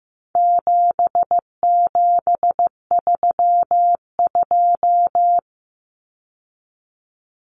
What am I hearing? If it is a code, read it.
Morse code, heard as a keyed tone, about 15 wpm, 700 Hz. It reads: 7732
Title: Common Cut Numbers - 4-Digit 15wpm